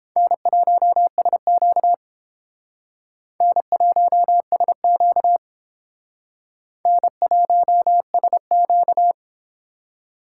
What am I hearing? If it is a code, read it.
Morse code, heard as a keyed tone, about 30 wpm, 700 Hz. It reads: N1SQ N1SQ N1SQ